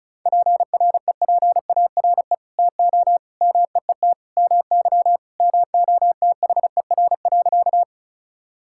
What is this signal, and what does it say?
Morse code, heard as a keyed tone, about 36 wpm, 700 Hz. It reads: PREPARE TO MEET MY MOTHER.
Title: Sentences from Top 1000 Words - Review Part 6 - 35wpm